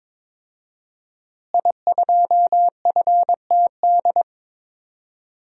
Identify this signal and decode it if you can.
Morse code, heard as a keyed tone, about 23 wpm, 700 Hz. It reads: I2FTD
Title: DXCC-100 Call Signs - 5 Characters - Repeated 3x - 22wpm